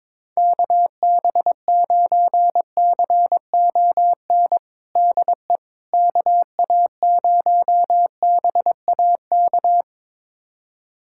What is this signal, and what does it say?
Morse code, heard as a keyed tone, about 22 wpm, 700 Hz. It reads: KB9CON DE KA0BAK
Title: USA Callsign DE Callsign 22wpm